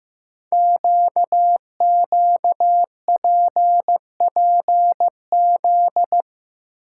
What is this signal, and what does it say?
Morse code, heard as a keyed tone, about 15 wpm, 700 Hz. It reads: QQPPZ
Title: Code Groups - 5 Letters - 15wpm